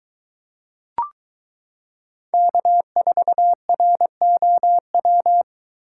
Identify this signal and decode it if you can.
Morse code, heard as a keyed tone, about 23 wpm, 700 Hz. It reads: K4ROW